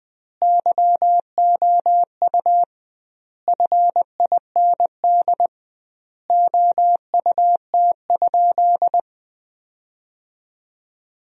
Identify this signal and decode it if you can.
Morse code, heard as a keyed tone, about 20 wpm, 700 Hz. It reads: YOU FIND OUT?
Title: 2x Word-Spacing - Sentences from Top 100 Words - 20wpm